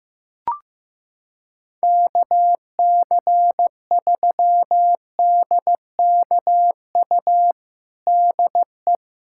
Morse code, heard as a keyed tone, about 15 wpm, 700 Hz. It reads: KC3DKU DE
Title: USA Callsign DE Callsign 15wpm